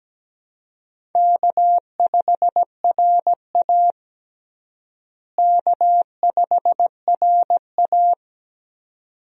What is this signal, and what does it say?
Morse code, heard as a keyed tone, about 17 wpm, 700 Hz. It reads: K5RA K5RA
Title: USA Call Signs - 4 Characters - Repeated 2x - 17wpm